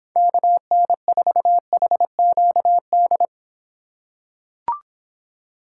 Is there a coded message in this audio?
Yes: Morse code from 0.2 to 3.2 s, heard as a keyed tone, about 25 wpm, 700 Hz. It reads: KN4HQD